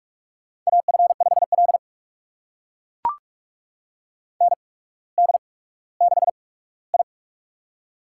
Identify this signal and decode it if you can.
Morse code, heard as a keyed tone, about 44 wpm, spaced as for 19 wpm, 700 Hz. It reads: AF5L ND6I
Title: USA Call Signs - 8x Character Spacing - 45wpm